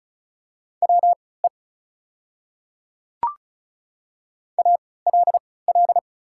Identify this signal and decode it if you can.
Morse code, heard as a keyed tone, about 36 wpm, spaced as for 20 wpm, 700 Hz. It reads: WE ALL
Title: 3x Character Spacing - Top 300 Words - 35wpm